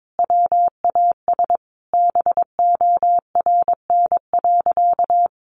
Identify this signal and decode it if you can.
Morse code, heard as a keyed tone, about 22 wpm, 700 Hz. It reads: WAS BORN.
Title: Sentences from Top 1300 Words 22wpm